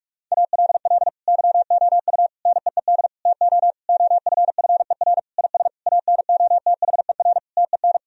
Morse code, heard as a keyed tone, about 46 wpm, 700 Hz. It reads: ALL YOU NEED TO OFFER IS ANOTHER TEN
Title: Sentences from Top 1000 Words - Review Part 4 - 45wpm